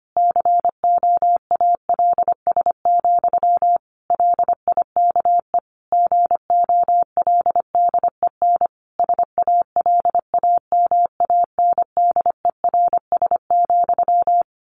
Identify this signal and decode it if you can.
Morse code, heard as a keyed tone, about 25 wpm, 700 Hz. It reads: COALS, LIKE GOLDEN SALAMANDERS,
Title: A Sound Of Thunder 25wpm